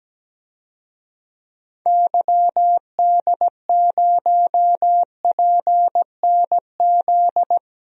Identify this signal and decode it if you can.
Morse code, heard as a keyed tone, about 17 wpm, 700 Hz. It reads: YD0PNZ
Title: DXCC-100 Call Signs - 6 Characters - 17wpm